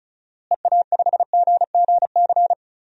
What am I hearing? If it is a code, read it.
Morse code, heard as a keyed tone, about 36 wpm, 700 Hz. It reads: EA5GGC
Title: DXCC-100 Call Signs - 6 Characters - Repeated 2x - 35wpm